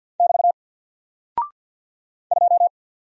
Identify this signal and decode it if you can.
Morse code, heard as a keyed tone, about 49 wpm, 700 Hz. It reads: <BT> 2
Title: Single Character (Letter:Number:Punctuation:Prosign) 50wpm